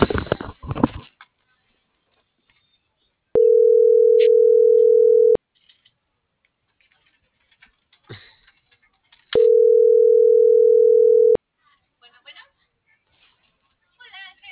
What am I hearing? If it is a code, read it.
Ringback tone (North America), 440 + 480 Hz.